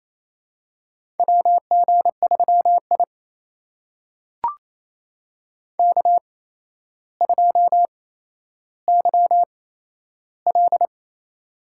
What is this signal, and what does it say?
Morse code, heard as a keyed tone, about 29 wpm, spaced as for 12 wpm, 700 Hz. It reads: WG3I K2YL
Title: USA Call Signs - 4 Characters - 8x Character Spacing - 28wpm